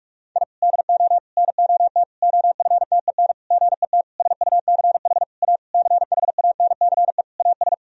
Morse code, heard as a keyed tone, about 41 wpm, 700 Hz. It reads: I DO NOT OFTEN GET SUCH A CHANCE AS
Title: Sentences from Top 1000 Words - Review Part 1 - 45wpm